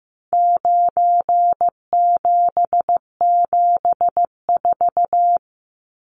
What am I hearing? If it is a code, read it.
Morse code, heard as a keyed tone, about 15 wpm, 700 Hz. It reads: 9774
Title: Numbers - 4-Digit 15wpm